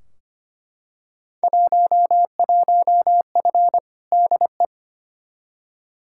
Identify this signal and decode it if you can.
Morse code, heard as a keyed tone, about 25 wpm, 700 Hz. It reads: 11F DE
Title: Field Day Exchanges Encore - 25wpm